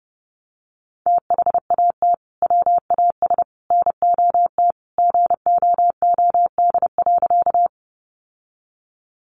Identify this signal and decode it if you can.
Morse code, heard as a keyed tone, about 30 wpm, 700 Hz. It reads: THAT WAS NOT GOOD.
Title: Sentences from Top 600 Words 30wpm